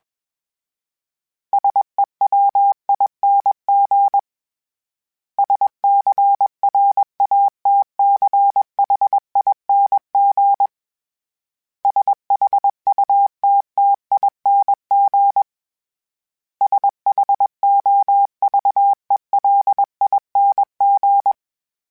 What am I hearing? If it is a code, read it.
Morse code, heard as a keyed tone, about 21 wpm, 800 Hz. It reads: SEWING SCRATCHING SHUTTING SHOVELING
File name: Special INGs / Special INGs-mp3s / Sngl Ing @20